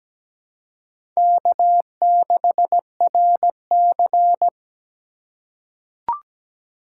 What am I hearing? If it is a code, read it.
Morse code, heard as a keyed tone, about 17 wpm, 700 Hz. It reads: K6RC
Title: K1USN SST QSO Practice - 17wpm